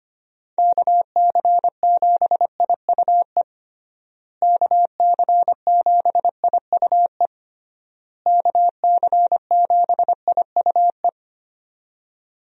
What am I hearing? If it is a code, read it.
Morse code, heard as a keyed tone, about 25 wpm, 700 Hz. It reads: KC7IUE KC7IUE KC7IUE
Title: USA Call Signs - 6 Characters - Repeated 3x - 25wpm